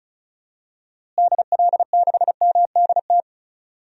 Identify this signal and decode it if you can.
Morse code, heard as a keyed tone, about 36 wpm, 700 Hz. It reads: DL6MDT
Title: DXCC-100 Call Signs - 6 Characters - Repeated 2x - 35wpm